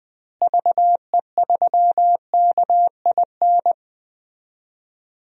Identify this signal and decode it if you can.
Morse code, heard as a keyed tone, about 20 wpm, 700 Hz. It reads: VE3KIN
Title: DXCC-100 Call Signs - 6 Characters - Repeated 3x - 20wpm